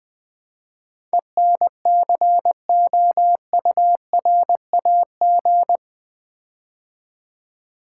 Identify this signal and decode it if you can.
Morse code, heard as a keyed tone, about 20 wpm, 700 Hz. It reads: ENCOURAG
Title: Build A Word - 4 Letters At A Time - 11-Letter Words - 20wpm